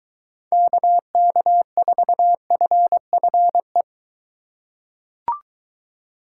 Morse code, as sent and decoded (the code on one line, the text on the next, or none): -.- -.- ....- ..-. ..-. .
KK4FFE